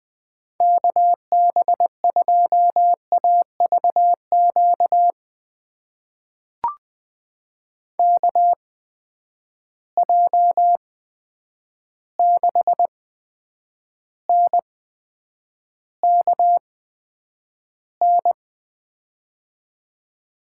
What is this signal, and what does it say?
Morse code, heard as a keyed tone, about 20 wpm, spaced as for 9 wpm, 700 Hz. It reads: KB2AVQ KJ6NKN
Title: USA Call Signs - 6 Characters - 8x Character Spacing - 20wpm